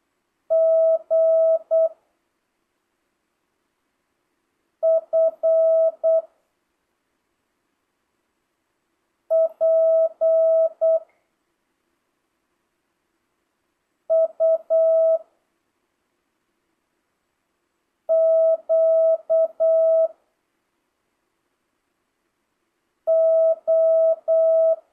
Morse code, heard as a keyed tone, about 8 wpm, 1300 Hz. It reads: G F P U Q O
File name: jh8cbh / 2025cw